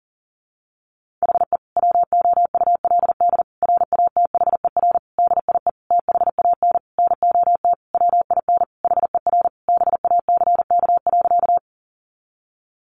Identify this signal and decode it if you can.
Morse code, heard as a keyed tone, about 40 wpm, 700 Hz. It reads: HE WOULD RATHER DIE THAN NOT WIN HER BACK.
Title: Sentences from Top 1500 Words 40wpm